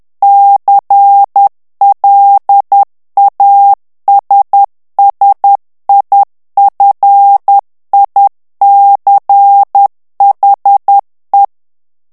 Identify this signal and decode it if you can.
Morse code, heard as a keyed tone, about 10 wpm, 800 Hz. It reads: CLASSIFICHE
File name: ITALIEN texte clair 2 vitesse 10 mots minute